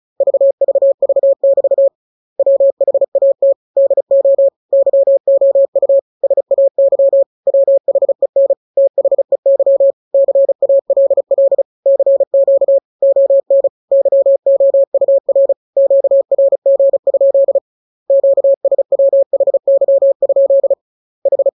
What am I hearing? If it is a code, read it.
Morse code, heard as a keyed tone, about 34 wpm, 550 Hz. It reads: VVV<BT> WHAT DO YOU SAY WHEN THEY CALL CQ ON YOUR QRG? QSWHY? H
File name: jokes / joke 21 35 0000